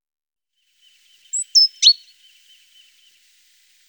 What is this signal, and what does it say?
There is no code.